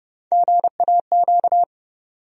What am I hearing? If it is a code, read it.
Morse code, heard as a keyed tone, about 30 wpm, 700 Hz. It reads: GAQ
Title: USA Call Sign Postfixes - 3 Letters 30wpm